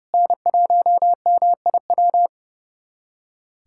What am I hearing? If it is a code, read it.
Morse code, heard as a keyed tone, about 30 wpm, 700 Hz. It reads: N1MIW
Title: USA Callsign DE Callsign - 5 Characters - 30wpm